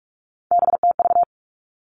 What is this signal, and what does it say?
Morse code, heard as a keyed tone, about 46 wpm, 700 Hz. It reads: BTV
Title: Three Challenging Letter-Number Combinations 45wpm